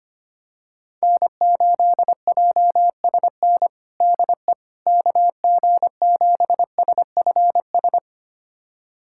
Morse code, heard as a keyed tone, about 25 wpm, 700 Hz. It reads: N8JSN DE KG7SFS
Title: USA Callsign DE Callsign 25wpm